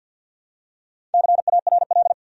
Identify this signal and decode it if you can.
Morse code, heard as a keyed tone, about 52 wpm, 700 Hz. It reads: CARL
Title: USA Common Names - Max 5 Letters - 50wpm